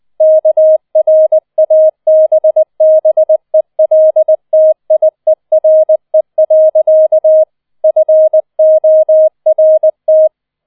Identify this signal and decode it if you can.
Morse code, heard as a keyed tone, about 20 wpm, 630 Hz. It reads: KRABBELTIERE. FORT
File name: 20wpm